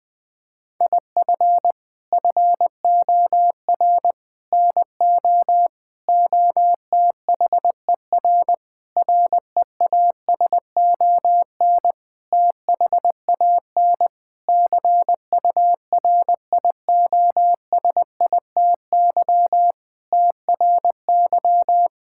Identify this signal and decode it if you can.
Morse code, heard as a keyed tone, about 19 wpm, 700 Hz. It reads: IF FOR NO OTHER REASON THAN CURIOSITY TRY
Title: Sentences with 6-Word Phrases - 20wpm